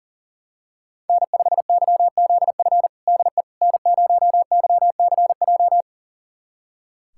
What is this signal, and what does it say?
Morse code, heard as a keyed tone, about 40 wpm, 700 Hz. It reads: N5YZF DE N0YCJ
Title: USA Callsign DE Callsign - 5 Characters - 40wpm